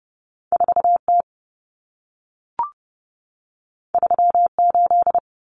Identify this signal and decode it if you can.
Morse code, heard as a keyed tone, about 30 wpm, 700 Hz. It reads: 4T 38
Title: Common Cut Numbers - 2-Digit 30wpm